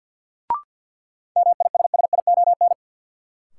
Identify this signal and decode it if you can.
Morse code, heard as a keyed tone, about 52 wpm, 700 Hz. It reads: MISSION
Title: Sentences from Top 1600 Words - New Word Drill - 50wpm